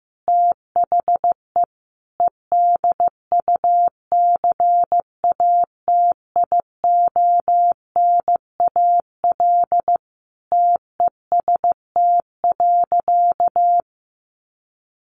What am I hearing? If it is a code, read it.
Morse code, heard as a keyed tone, about 15 wpm, 700 Hz. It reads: THE EDUCATIONAL TEST.
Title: Sentences from Top 1200 Words 15wpm